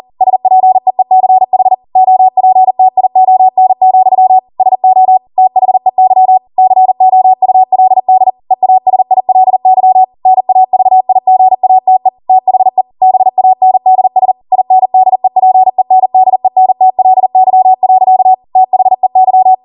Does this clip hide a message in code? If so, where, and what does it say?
Yes: Morse code from 0.2 to 19.5 s, heard as a keyed tone, about 40 wpm, 750 Hz. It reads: SPEECH OPTION, SO THEY COULD EASILY NAVIGATE THE BANDS INDEPENDENTLY. THEY